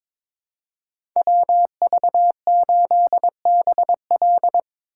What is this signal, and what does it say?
Morse code, heard as a keyed tone, about 22 wpm, 700 Hz. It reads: WV8BL
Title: USA Call Signs - 5 Characters - 4x Character Spacing - 22wpm